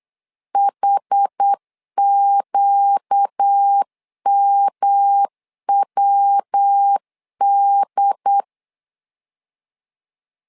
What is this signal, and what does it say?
Morse code, heard as a keyed tone, about 8 wpm, 800 Hz. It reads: HQMWD
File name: 4 Q 8wpm